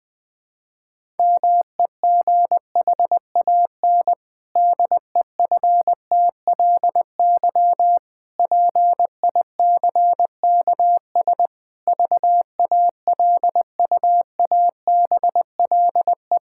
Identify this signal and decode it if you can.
Morse code, heard as a keyed tone, about 20 wpm, 700 Hz. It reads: MEGHAN DEFTLY PICKS VALUABLE
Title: Pangram Challenge v2 - 20wpm